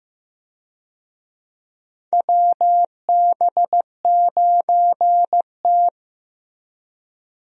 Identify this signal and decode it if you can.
Morse code, heard as a keyed tone, about 15 wpm, 700 Hz. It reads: WB9T
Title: USA Call Signs - 4 Characters - Repeated 2x - 15wpm